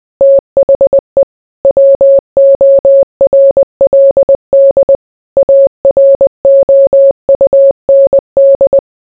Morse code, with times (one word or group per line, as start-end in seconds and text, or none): 0.2-1.2 THE
1.6-5.0 WORLD
5.4-8.8 AROUND